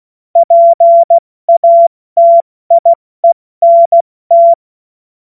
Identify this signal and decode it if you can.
Morse code, heard as a keyed tone, about 16 wpm, 680 Hz. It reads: PATIENT
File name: Morsekurs / Witze / Witz 25. Juli 2023 70 F80